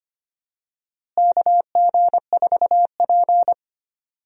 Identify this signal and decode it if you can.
Morse code, heard as a keyed tone, about 25 wpm, 700 Hz. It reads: KG4P
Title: USA Call Signs - 4 Characters - Repeated 3x - 25wpm